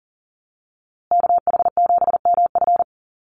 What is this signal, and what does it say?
Morse code, heard as a keyed tone, about 40 wpm, 700 Hz. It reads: KH7MF
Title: USA Call Signs Repeated 2x 40wpm